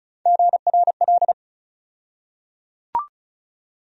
Morse code, heard as a keyed tone, about 34 wpm, 700 Hz. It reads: GRL
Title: USA Call Sign Postfixes - 3 Letters 35wpm